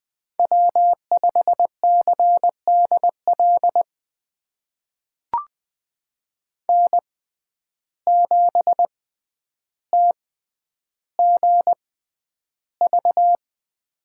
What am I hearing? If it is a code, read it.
Morse code, heard as a keyed tone, about 20 wpm, spaced as for 10 wpm, 700 Hz. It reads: W5CDL N7TGV
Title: USA Call Signs - 5 Characters - 6x Character Spacing - 20wpm